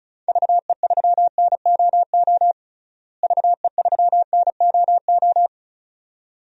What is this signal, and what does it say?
Morse code, heard as a keyed tone, about 34 wpm, 700 Hz. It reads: VE3NOO VE3NOO
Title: DXCC-100 Call Signs - 6 Characters - Repeated 2x - 35wpm